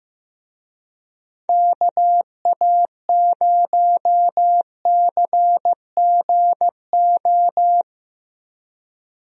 Morse code, heard as a keyed tone, about 16 wpm, 700 Hz. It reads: KA0CGO
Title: USA Call Signs - 6 Characters - Repeated 3x - 15wpm